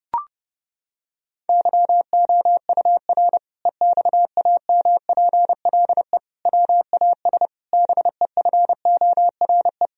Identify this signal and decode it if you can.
Morse code, heard as a keyed tone, about 30 wpm, 700 Hz. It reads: YOUR EXAMPLE WAS BEFORE
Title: Sentences from Top 1000 Words - Review Part 1 - 30wpm